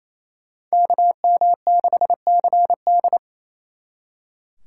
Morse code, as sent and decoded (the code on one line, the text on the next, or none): -.- -- -.... -.-. -..
KM6CD